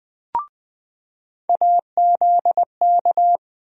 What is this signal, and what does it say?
Morse code, heard as a keyed tone, about 20 wpm, 700 Hz. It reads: AZK
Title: USA Call Sign Postfixes - 3 Letters 20wpm